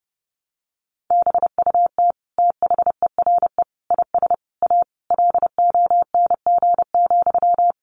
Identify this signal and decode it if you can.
Morse code, heard as a keyed tone, about 28 wpm, 700 Hz. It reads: BUT THERE IS A LONG,
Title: Sentences from Top 1100 Words 30wpm